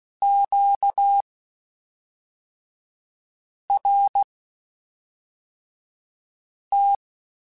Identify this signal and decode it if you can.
Morse code, heard as a keyed tone, about 16 wpm, spaced as for 3 wpm, 800 Hz. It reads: QRT
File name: (spaced) / Q-Calls Spaced-mp3s / Q Calls Twice sp @5